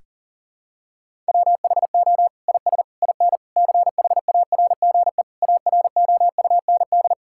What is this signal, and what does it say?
Morse code, heard as a keyed tone, about 40 wpm, 700 Hz. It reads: WHO IS IN CHARGE AROUND
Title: Sentences from Top 1000 Words - Review Part 6 - 40wpm